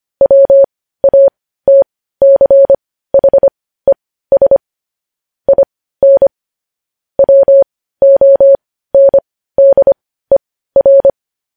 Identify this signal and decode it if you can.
Morse code, heard as a keyed tone, about 25 wpm, spaced as for 15 wpm, 550 Hz. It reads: WATCHES IN WONDER